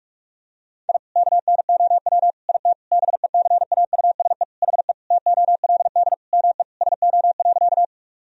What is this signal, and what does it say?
Morse code, heard as a keyed tone, about 46 wpm, 700 Hz. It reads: I KNOW IT BECAUSE HE TOLD ME SO.
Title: Sentences from Top 1000 Words - Review Part 3 - 45wpm